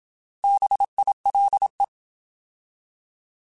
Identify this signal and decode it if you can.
Morse code, heard as a keyed tone, about 27 wpm, 800 Hz. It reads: BILE
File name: Alphabetical / Alphabetical mp3s / Bs @25